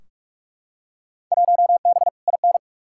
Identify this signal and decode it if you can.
Morse code, heard as a keyed tone, about 46 wpm, 700 Hz. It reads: 1B IN
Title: Field Day Exchanges - 45wpm